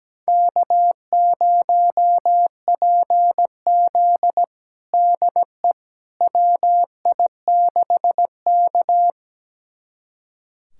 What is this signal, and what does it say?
Morse code, heard as a keyed tone, about 17 wpm, 700 Hz. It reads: K0PZ DE WI6K
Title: USA Callsign DE Callsign - 4 Characters - 17wpm